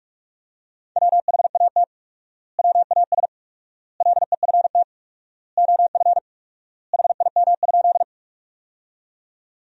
Morse code, heard as a keyed tone, about 47 wpm, 700 Hz. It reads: WHAT WAS LEFT OF HIM?
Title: 4x Word-Spacing - Sentences from Top 300 Words - 45wpm